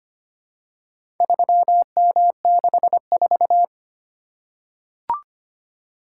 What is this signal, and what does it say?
Morse code, heard as a keyed tone, about 25 wpm, 700 Hz. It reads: 3M64
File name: practice / 25wpm / Four Letter-Number Combinations - 25wpm